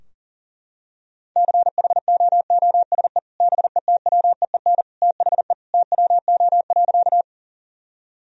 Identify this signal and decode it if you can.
Morse code, heard as a keyed tone, about 40 wpm, 700 Hz. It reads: CHOOSE BETWEEN THE TWO.
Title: Sentences from Top 1000 Words - Review Part 4 - 40wpm